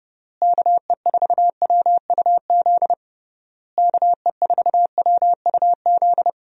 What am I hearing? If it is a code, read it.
Morse code, heard as a keyed tone, about 30 wpm, 700 Hz. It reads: KE4WUZ KE4WUZ
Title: USA Call Signs - 6 Characters - Repeated 3x - 30wpm